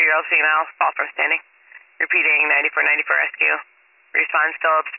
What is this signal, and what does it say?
Emergency Alert System (SAME) tones, heard as harsh buzzing bursts, with speech. It is heard from 1.0 to 4.6 s.